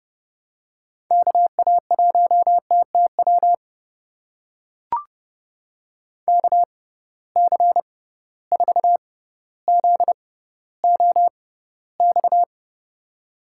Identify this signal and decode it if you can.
Morse code, heard as a keyed tone, about 30 wpm, spaced as for 15 wpm, 700 Hz. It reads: KA1TTW KC4ZOX
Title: USA Call Signs - 6 Characters - 6x Character Spacing - 30wpm